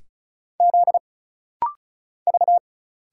Morse code, heard as a keyed tone, about 34 wpm, 700 Hz. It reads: Z V
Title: Single Letter-Number - Rapid-Fire 35wpm